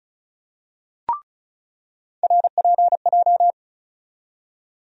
Morse code, heard as a keyed tone, about 34 wpm, 700 Hz. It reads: RPJ
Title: USA Call Sign Postfixes - 3 Letters 35wpm